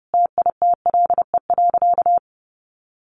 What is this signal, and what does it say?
Morse code, heard as a keyed tone, about 30 wpm, 700 Hz. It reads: TITLE.
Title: Sentences from Top 1200 Words 30wpm